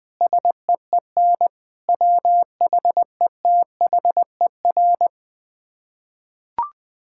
Morse code, heard as a keyed tone, about 20 wpm, 700 Hz. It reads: SEEN WHETHER
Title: Sets of 6 Words - 20wpm